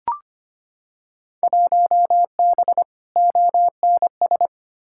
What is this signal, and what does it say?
Morse code, heard as a keyed tone, about 25 wpm, 700 Hz. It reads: 1B ONS
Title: Field Day Exchanges Encore - 25wpm